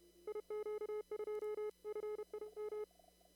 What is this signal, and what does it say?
Morse code, heard as a keyed tone, about 31 wpm, 420 Hz. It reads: IQ2FIM
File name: CW / kt5j wpx cw-op-k5pi